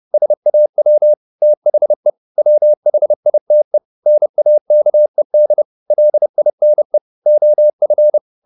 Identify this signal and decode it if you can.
Morse code, heard as a keyed tone, about 30 wpm, 600 Hz. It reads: SAW THE WHITE NAKED LINE OF